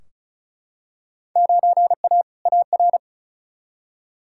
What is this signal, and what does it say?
Morse code, heard as a keyed tone, about 34 wpm, 700 Hz. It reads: 9A AR
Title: Field Day Exchanges Encore - 35wpm